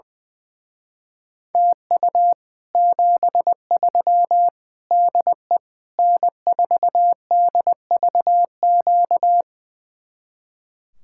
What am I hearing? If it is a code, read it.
Morse code, heard as a keyed tone, about 20 wpm, 700 Hz. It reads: TU 73 DE N4DVQ
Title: Speed-Racing Common QSO Phrases - 20wpm